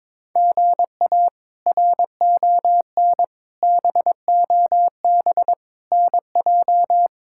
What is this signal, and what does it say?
Morse code, heard as a keyed tone, about 22 wpm, 700 Hz. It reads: GA RON BOB NJ
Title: K1USN SST QSO Practice - 22wpm